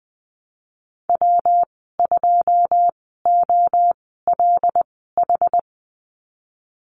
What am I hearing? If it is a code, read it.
Morse code, heard as a keyed tone, about 20 wpm, spaced as for 15 wpm, 700 Hz. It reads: W2OLH
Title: 147 Lesson 24g - Letter y - Call Signs 20wpm